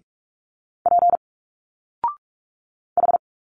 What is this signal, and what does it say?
Morse code, heard as a keyed tone, about 46 wpm, 700 Hz. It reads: P H
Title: Single Letters 45wpm